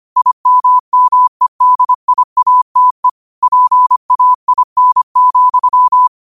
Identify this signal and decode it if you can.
Morse code, heard as a keyed tone, about 25 wpm, 1000 Hz. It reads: IMMEDIATE PAIN,